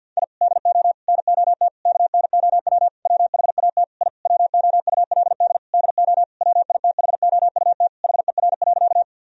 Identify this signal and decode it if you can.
Morse code, heard as a keyed tone, about 50 wpm, 700 Hz. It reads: I DO NOT KNOW WHAT I WOULD DO WITHOUT HER.
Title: Sentences from Top 1000 Words - Review Part 3 - 50wpm